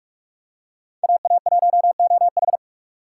Morse code, heard as a keyed tone, about 44 wpm, 700 Hz. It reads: AA1OH
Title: USA Call Signs - 5 Characters - 4x Character Spacing - 45wpm